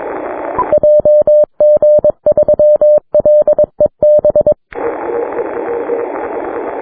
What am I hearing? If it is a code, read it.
Morse code, heard as a keyed tone, about 22 wpm, 600 Hz. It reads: JG3LEB
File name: wp / LA8AJA heard by JG3LEB on 6m